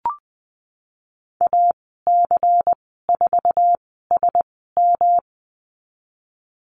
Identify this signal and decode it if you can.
Morse code, heard as a keyed tone, about 20 wpm, spaced as for 15 wpm, 700 Hz. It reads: AC4SM
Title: 128 Lesson 21g - Number 6 - Call Signs 20wpm